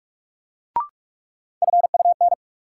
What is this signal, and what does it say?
Morse code, heard as a keyed tone, about 41 wpm, 700 Hz. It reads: FUN
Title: Sentences from Top 700 Words - New Word Drill - 45wpm